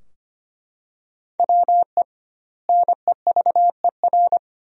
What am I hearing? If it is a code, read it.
Morse code, heard as a keyed tone, about 25 wpm, 700 Hz. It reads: WE NEVER
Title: 2x Word-Spacing - Sentences from Top 300 Words - 25wpm